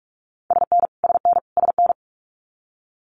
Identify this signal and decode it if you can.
Morse code, heard as a keyed tone, about 44 wpm, 700 Hz. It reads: SN SN SN
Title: Abbreviations Repeated 3x - 45wpm